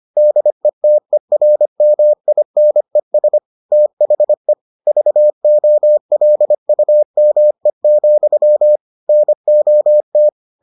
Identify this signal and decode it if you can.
Morse code, heard as a keyed tone, about 25 wpm, 600 Hz. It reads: DETERMINES THE VOLUME, NOT